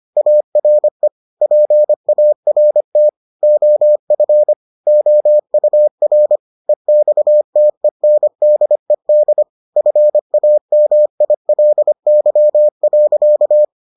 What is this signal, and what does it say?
Morse code, heard as a keyed tone, about 25 wpm, 600 Hz. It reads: ARE PART OF OUR EXTENDED FAMILY.